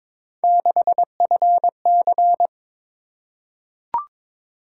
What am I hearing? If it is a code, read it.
Morse code, heard as a keyed tone, about 23 wpm, 700 Hz. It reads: 6FC
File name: practice / 22wpm / Three Letter-Number Combinations - 22wpm